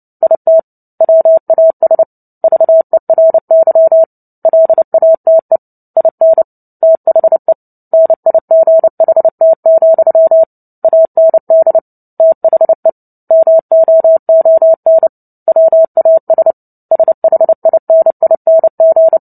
Morse code, heard as a keyed tone, about 30 wpm, 650 Hz. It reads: IT WAS VERY LATE IN THE NIGHT, AND THE MOON WAS SHINING
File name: Book Two / 03-Chapter 3 - 29 WPM